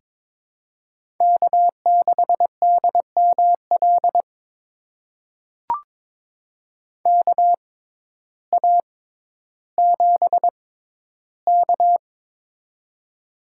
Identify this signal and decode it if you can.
Morse code, heard as a keyed tone, about 22 wpm, spaced as for 12 wpm, 700 Hz. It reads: K6DML KA7K
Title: USA Call Signs - 6x Character Spacing - 22wpm